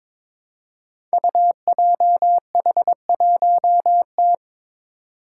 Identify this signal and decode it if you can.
Morse code, heard as a keyed tone, about 23 wpm, 700 Hz. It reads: UJH1T
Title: Five Challenging Letter-Number Combinations 22wpm